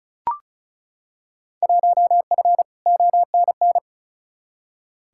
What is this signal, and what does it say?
Morse code, heard as a keyed tone, about 36 wpm, 700 Hz. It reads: 1F ONN
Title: Field Day Exchanges Encore - 35wpm